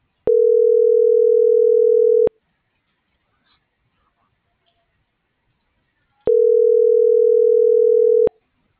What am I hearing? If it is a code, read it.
Ringback tone (North America), 440 + 480 Hz.